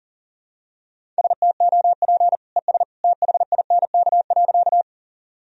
Morse code, heard as a keyed tone, about 40 wpm, 700 Hz. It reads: STOP ES THINK.
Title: Sentences from Top 300 Words and Abbreviations 40wpm